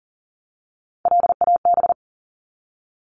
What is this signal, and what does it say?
Morse code, heard as a keyed tone, about 40 wpm, 700 Hz. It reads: LAB
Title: Top 1751-2000 Words 40wpm